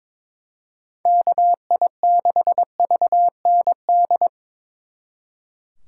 Morse code, heard as a keyed tone, about 25 wpm, 700 Hz. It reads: KI6VND
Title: Speed Racing - USA 2x3 Call Signs 22wpm